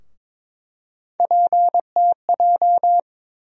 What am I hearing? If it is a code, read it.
Morse code, heard as a keyed tone, about 22 wpm, 700 Hz. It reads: PTJ
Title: USA Call Sign Postfixes - 3 Letters 22wpm